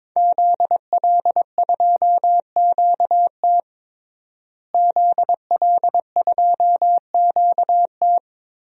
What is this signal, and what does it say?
Morse code, heard as a keyed tone, about 23 wpm, 700 Hz. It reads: ZL2QT ZL2QT
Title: DXCC-100 Call Signs - 5 Characters - Repeated 3x - 22wpm